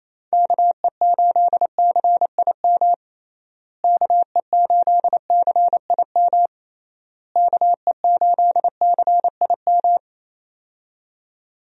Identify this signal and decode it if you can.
Morse code, heard as a keyed tone, about 28 wpm, 700 Hz. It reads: KE8CIM KE8CIM KE8CIM
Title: USA Call Signs - 6 Characters - Repeated 3x - 28wpm